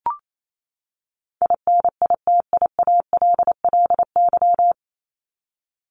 Morse code, heard as a keyed tone, about 28 wpm, 700 Hz. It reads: INITIALLY
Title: Top 1251-1500 Words 28wpm